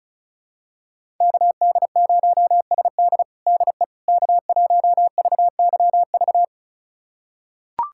Morse code, heard as a keyed tone, about 36 wpm, 700 Hz. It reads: KD0SD DE K1VYV
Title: USA Callsign DE Callsign 35wpm